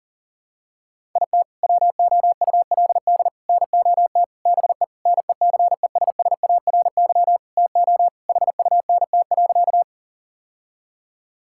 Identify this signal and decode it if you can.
Morse code, heard as a keyed tone, about 40 wpm, 700 Hz. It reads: IT WOULD NOT BE NECESSARY TO HUNT.
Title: Sentences from Top 1600 Words 40wpm